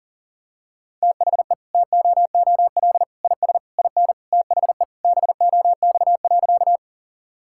Morse code, heard as a keyed tone, about 40 wpm, 700 Hz. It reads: THE TOOL IS IN THE BOX.
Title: Sentences from Top 1000 Words - Review Part 6 - 40wpm